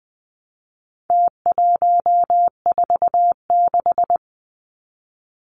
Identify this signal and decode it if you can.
Morse code, heard as a keyed tone, about 20 wpm, 700 Hz. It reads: T146
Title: Numbers - Contest Style - 4-Digit 20wpm